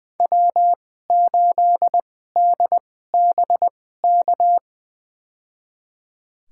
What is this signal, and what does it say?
Morse code, heard as a keyed tone, about 20 wpm, spaced as for 14 wpm, 700 Hz. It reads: W8DBK
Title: USA Call Signs - 5 Characters - 2x Character Spacing - 20wpm